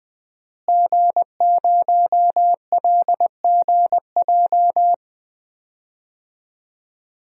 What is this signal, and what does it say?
Morse code, heard as a keyed tone, about 20 wpm, 700 Hz. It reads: G0LGJ
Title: DXCC-100 Call Signs - 5 Characters - Repeated 2x - 20wpm